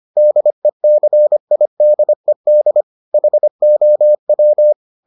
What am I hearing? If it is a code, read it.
Morse code, heard as a keyed tone, about 25 wpm, 600 Hz. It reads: DECIDED HOW